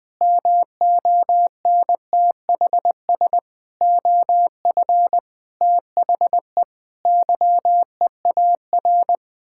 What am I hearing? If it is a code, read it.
Morse code, heard as a keyed tone, about 20 wpm, 700 Hz. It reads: MONTHS OF THE YEAR
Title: Sets of 8 Words - 20wpm